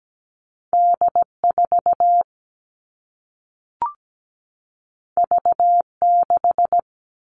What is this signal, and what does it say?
Morse code, heard as a keyed tone, about 17 wpm, 700 Hz. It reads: D4 V6
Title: All Cut Numbers - 2-Digit 17wpm